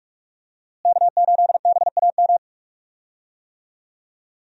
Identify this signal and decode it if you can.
Morse code, heard as a keyed tone, about 46 wpm, 700 Hz. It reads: K8BAM
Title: USA Call Signs - 5 Characters - Repeated 2x - 45wpm